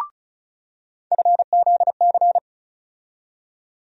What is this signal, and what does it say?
Morse code, heard as a keyed tone, about 34 wpm, 700 Hz. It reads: FZC
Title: USA Call Sign Postfixes - 3 Letters 35wpm